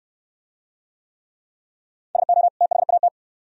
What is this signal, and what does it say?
Morse code, heard as a keyed tone, about 59 wpm, 700 Hz. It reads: SO THAT